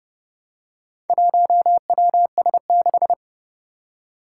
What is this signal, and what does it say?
Morse code, heard as a keyed tone, about 29 wpm, 700 Hz. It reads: 1WS6
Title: Four Challenging Letter-Number Combinations 30wpm